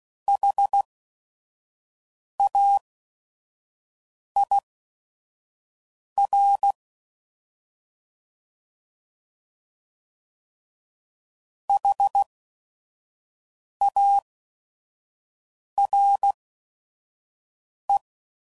Morse code, heard as a keyed tone, about 16 wpm, spaced as for 5 wpm, 800 Hz. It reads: HAIR HARE
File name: Prefixes / By First Letter / Hs